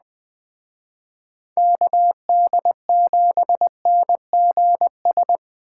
Morse code, heard as a keyed tone, about 20 wpm, 700 Hz. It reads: KD7NGS
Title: Speed Racing - USA 2x3 Call Signs 17wpm